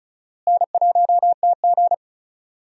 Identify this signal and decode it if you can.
Morse code, heard as a keyed tone, about 36 wpm, 700 Hz. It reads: N1TG
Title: USA Call Signs - 4 Characters - Repeated 3x - 35wpm